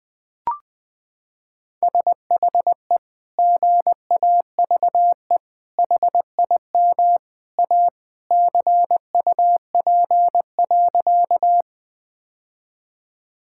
Morse code, heard as a keyed tone, about 21 wpm, 700 Hz. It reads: SHE GAVE HIM A CUP.
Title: Sentences from Top 1600 Words 20wpm